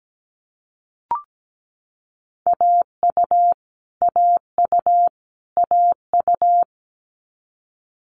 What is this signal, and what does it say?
Morse code, heard as a keyed tone, about 17 wpm, 700 Hz. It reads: AU AU AU
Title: Multi-letter Phonemes Repeated 3x 17wpm